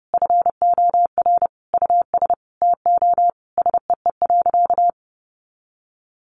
Morse code, heard as a keyed tone, about 30 wpm, 700 Hz. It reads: FOR US TO SEE.
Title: Sentences from Top 500 Words 30wpm